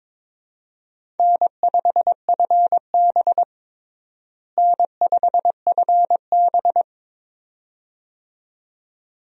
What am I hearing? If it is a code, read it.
Morse code, heard as a keyed tone, about 22 wpm, 700 Hz. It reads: N5FB N5FB
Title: USA Call Signs - 4 Characters - Repeated 2x - 22wpm